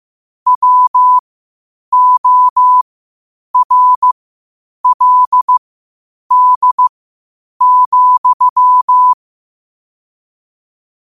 Morse code, heard as a keyed tone, about 15 wpm, spaced as for 8 wpm, 1000 Hz. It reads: WORLD,
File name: qotdfiles / 8wpm